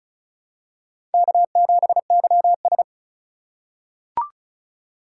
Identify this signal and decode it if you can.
Morse code, heard as a keyed tone, about 36 wpm, 700 Hz. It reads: K7YS
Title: USA Call Signs - 4 Characters - 3x Character Spacing - 35wpm